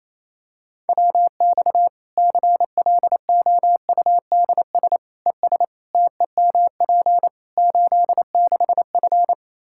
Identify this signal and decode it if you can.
Morse code, heard as a keyed tone, about 28 wpm, 700 Hz. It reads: WX CLOUDS ES TEMP 86F
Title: Practice QSO - Exchange 2 - Info - 28wpm